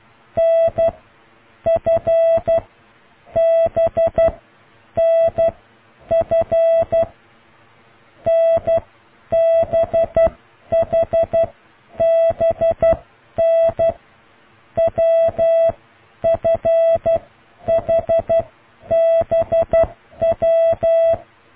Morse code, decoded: NFBNF NBHBN WFHBW